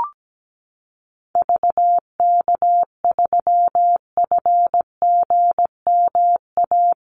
Morse code, heard as a keyed tone, about 17 wpm, 700 Hz. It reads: VK3FGMA
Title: USA Call Signs Repeated 2x 17wpm